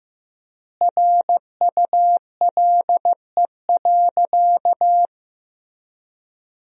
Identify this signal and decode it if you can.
Morse code, heard as a keyed tone, about 15 wpm, 700 Hz. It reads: RULE.
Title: 3x Word-Spacing - Sentences from Top 400 Words - 15wpm